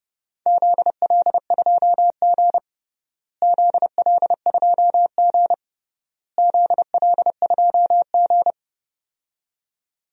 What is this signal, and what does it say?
Morse code, heard as a keyed tone, about 29 wpm, 700 Hz. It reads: ZL2G ZL2G ZL2G
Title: DXCC-100 Call Signs - 4 Characters - Repeated 3x - 30wpm